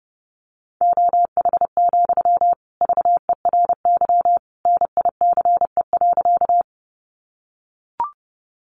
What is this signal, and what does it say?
Morse code, heard as a keyed tone, about 30 wpm, 700 Hz. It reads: OH, VERY NICE.
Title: Sentences from Top 1100 Words 30wpm